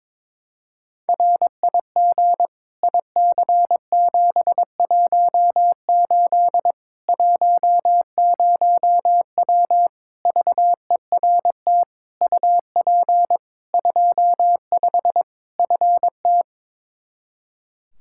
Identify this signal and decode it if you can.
Morse code, heard as a keyed tone, about 22 wpm, 700 Hz. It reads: RIG IC718 10W VERT UP 25 FT
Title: Practice QSO - Exchange 2 - Info - 22wpm